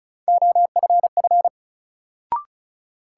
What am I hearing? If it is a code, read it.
Morse code, heard as a keyed tone, about 36 wpm, 700 Hz. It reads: OFF
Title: Sets of 2 Words - New Word Drill - 35wpm